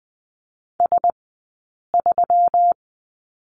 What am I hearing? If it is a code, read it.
Morse code, heard as a keyed tone, about 20 wpm, 700 Hz. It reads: S 3
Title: 120 Lesson 20d - Number 3 - Two Characters 20wpm